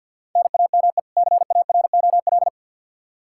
Morse code, heard as a keyed tone, about 52 wpm, 700 Hz. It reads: NAME CAROL
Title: K1USN SST QSO Practice 50wpm